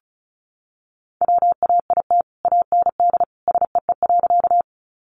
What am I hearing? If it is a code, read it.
Morse code, heard as a keyed tone, about 34 wpm, 700 Hz. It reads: WAIT AND SEE.
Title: Sentences from Top 500 Words 35wpm